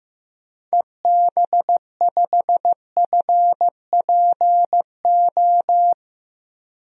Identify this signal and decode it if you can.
Morse code, heard as a keyed tone, about 15 wpm, 700 Hz. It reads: EB5FPO
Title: DXCC-100 Call Signs - 6 Characters - Repeated 2x - 15wpm